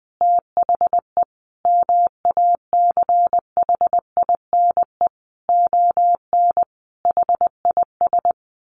Morse code, heard as a keyed tone, about 20 wpm, 700 Hz. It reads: THE MACHINE ON HIS
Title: A Sound Of Thunder 20wpm